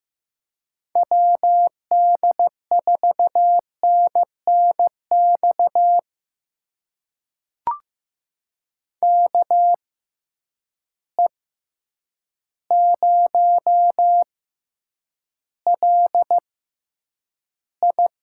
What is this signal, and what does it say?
Morse code, heard as a keyed tone, about 15 wpm, spaced as for 8 wpm, 700 Hz. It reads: WD4NNX KE0LI
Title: USA Call Signs - 6 Characters - 6x Character Spacing - 15wpm